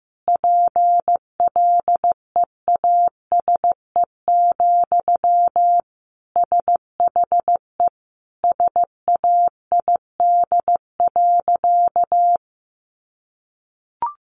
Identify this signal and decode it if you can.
Morse code, heard as a keyed tone, about 15 wpm, 700 Hz. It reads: PLEASE, SHE SAID.
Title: Sentences from Top 1500 Words 15wpm